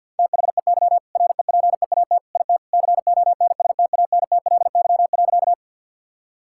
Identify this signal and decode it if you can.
Morse code, heard as a keyed tone, about 50 wpm, 700 Hz. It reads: THEY REPEAT IT CONSTANTLY.
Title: Sentences from Top 1600 Words 50wpm